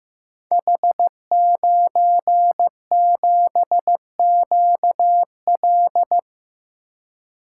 Morse code, heard as a keyed tone, about 15 wpm, 700 Hz. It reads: H97QL
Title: Code Groups - 5 Letter:Numbers - 15wpm